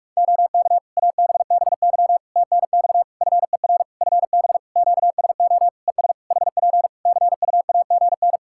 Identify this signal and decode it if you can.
Morse code, heard as a keyed tone, about 42 wpm, 700 Hz. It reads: OK ABBY TNX FER FB QSO ES HP CUAGN
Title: Practice QSO - Exchange 4 - Ending - 45wpm